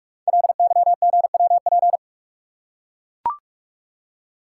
Morse code, heard as a keyed tone, about 44 wpm, 700 Hz. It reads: LYGWP
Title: Code Groups - 5 Letter:Numbers - 45wpm